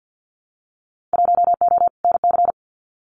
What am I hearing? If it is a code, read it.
Morse code, heard as a keyed tone, about 50 wpm, 700 Hz. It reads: 1O NC
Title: Winter Field Day Exchanges 50wpm